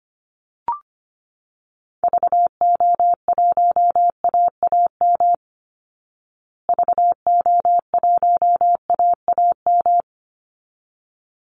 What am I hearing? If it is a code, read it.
Morse code, heard as a keyed tone, about 25 wpm, 700 Hz. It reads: VO1AAM VO1AAM
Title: Call Signs Repeated 3x - 25wpm